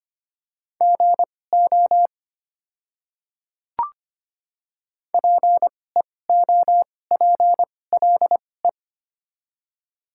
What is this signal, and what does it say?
Morse code, heard as a keyed tone, about 25 wpm, spaced as for 18 wpm, 690 Hz. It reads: GO PEOPLE
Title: 2x Character Spacing - Top 100 Words - 25wpm